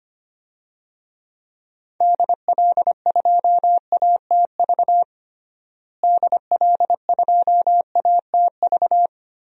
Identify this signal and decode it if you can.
Morse code, heard as a keyed tone, about 25 wpm, 700 Hz. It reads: DL2ATV DL2ATV
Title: DXCC-100 Call Signs - 6 Characters - Repeated 3x - 25wpm